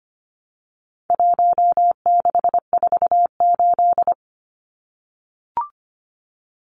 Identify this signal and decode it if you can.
Morse code, heard as a keyed tone, about 25 wpm, 700 Hz. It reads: 1648
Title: Numbers - 4-Digit 25wpm